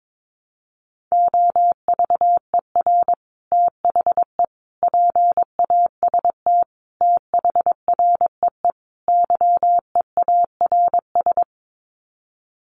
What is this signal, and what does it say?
Morse code, heard as a keyed tone, about 22 wpm, 700 Hz. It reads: OVER THE PAST THREE YEARS
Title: Sets of 5 Words Encore2 22wpm